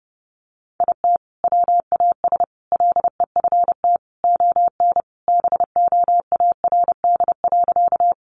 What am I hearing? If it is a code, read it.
Morse code, heard as a keyed tone, about 30 wpm, 700 Hz. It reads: IT WAS LEFT ON BOARD.
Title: Sentences from Top 700 Words 30wpm